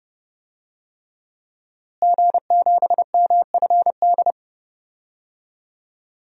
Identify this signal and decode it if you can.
Morse code, heard as a keyed tone, about 30 wpm, 700 Hz. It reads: G7MFD
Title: DXCC-100 Call Signs - 5 Characters - Repeated 3x - 30wpm